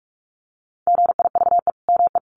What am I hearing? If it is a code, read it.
Morse code, heard as a keyed tone, about 44 wpm, 700 Hz. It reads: GIVE ME
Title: Sentences from Top 1500 Words 45wpm